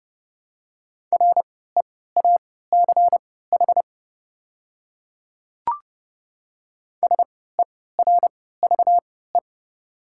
Morse code, heard as a keyed tone, about 31 wpm, spaced as for 17 wpm, 700 Hz. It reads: REACH SERVE
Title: 3x Character Spacing - Top 500 Words - 30wpm